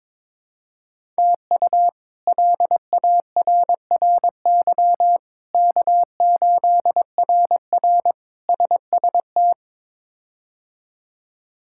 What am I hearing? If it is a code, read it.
Morse code, heard as a keyed tone, about 21 wpm, 700 Hz. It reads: TU LARRY K8RR SST
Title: K1USN SST QSO Practice - 22wpm